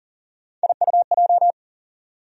Morse code, heard as a keyed tone, about 40 wpm, 700 Hz. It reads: IUJ
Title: USA Call Sign Postfixes - 3 Letters 40wpm